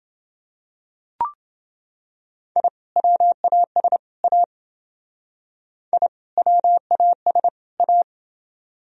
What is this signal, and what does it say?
Morse code, heard as a keyed tone, about 28 wpm, 700 Hz. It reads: I WAS A I WAS A